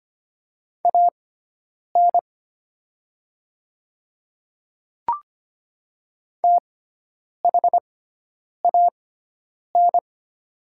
Morse code, heard as a keyed tone, about 25 wpm, spaced as for 9 wpm, 700 Hz. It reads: AN THAN
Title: 6x Character Spacing - Top 500 Words - 25wpm